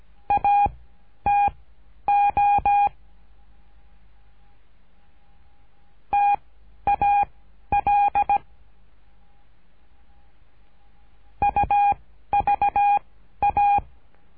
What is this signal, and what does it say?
Morse code, heard as a keyed tone, about 17 wpm, spaced as for 11 wpm, 810 Hz. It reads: ATO TAL UVA
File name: cursos / CURSO DE TELEGRAFIA / Telegrafia - Trilha 13